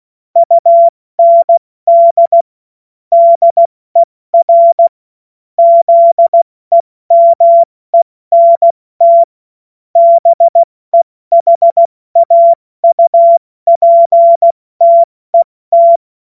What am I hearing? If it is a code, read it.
Morse code, heard as a keyed tone, about 16 wpm, 680 Hz. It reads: UND DER ZEMENT BEHAUPTET
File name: Morsekurs/Witze/Witz 12 Dezember 2023 70 F80